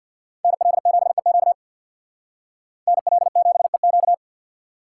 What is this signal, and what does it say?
Morse code, heard as a keyed tone, about 62 wpm, 700 Hz. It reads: NL7EQ NL7EQ
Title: Speed Racing - USA 2x2 Call Signs 50wpm